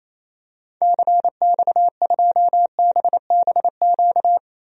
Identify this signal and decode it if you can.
Morse code, heard as a keyed tone, about 28 wpm, 700 Hz. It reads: CX2BBQ
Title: DXCC-100 Call Signs - 6 Characters - 28wpm